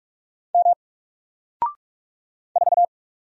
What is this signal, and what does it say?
Morse code, heard as a keyed tone, about 46 wpm, 700 Hz. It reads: M 4
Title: Single Letter-Number Alt 45wpm